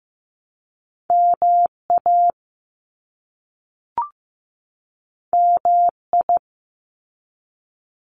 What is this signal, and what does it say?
Morse code, heard as a keyed tone, about 15 wpm, 700 Hz. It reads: MA MI
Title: US States 15wpm